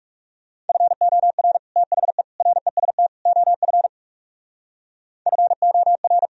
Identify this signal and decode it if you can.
Morse code, heard as a keyed tone, about 35 wpm, 700 Hz. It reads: FOR THE REST OF FOR
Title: Speed-Racing - Sets of 4 Words 30wpm